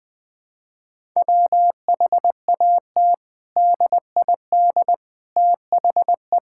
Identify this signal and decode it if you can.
Morse code, heard as a keyed tone, about 20 wpm, 700 Hz. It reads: WHAT DID THE
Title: Sentences from Top 1000 Words - Review Part 6 - 20wpm